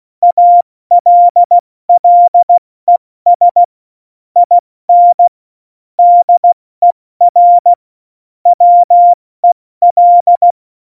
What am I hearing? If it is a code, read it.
Morse code, heard as a keyed tone, about 16 wpm, 700 Hz. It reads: ALLES IN DER WEL
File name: Morsekurs / Witze / Witz 20 March 2018, 70 F80